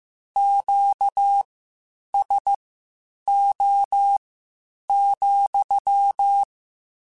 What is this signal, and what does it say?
Morse code, heard as a keyed tone, about 15 wpm, spaced as for 8 wpm, 800 Hz. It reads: QSO,
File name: Q Calls / Q Calls-mp3s / Q Calls sppc @13